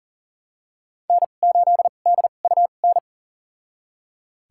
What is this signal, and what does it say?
Morse code, heard as a keyed tone, about 40 wpm, spaced as for 29 wpm, 700 Hz. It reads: N8DUN
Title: USA Call Signs - 5 Characters - 2x Character Spacing - 40wpm